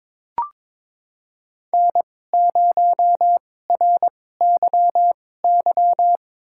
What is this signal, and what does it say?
Morse code, heard as a keyed tone, about 22 wpm, spaced as for 16 wpm, 700 Hz. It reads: N0RYY
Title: USA Call Signs - 5 Characters - 2x Character Spacing - 22wpm